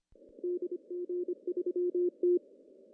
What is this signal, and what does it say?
Morse code, heard as a keyed tone, about 27 wpm, 340 Hz. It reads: DG3T
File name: CW / cqww-cw-k5tr